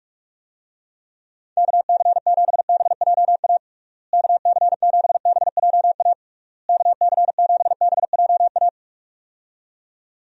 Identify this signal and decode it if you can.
Morse code, heard as a keyed tone, about 46 wpm, 700 Hz. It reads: KC7BJA KC7BJA KC7BJA
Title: USA Call Signs - 6 Characters - Repeated 3x - 45wpm